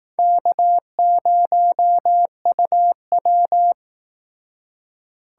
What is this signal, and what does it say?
Morse code, heard as a keyed tone, about 19 wpm, 700 Hz. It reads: K0UW